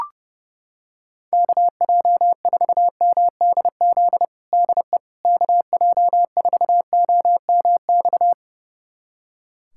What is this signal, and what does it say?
Morse code, heard as a keyed tone, about 30 wpm, 700 Hz. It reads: KJ4MDZ DE KJ4OMX
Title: USA Callsign DE Callsign - 6 Characters - 30wpm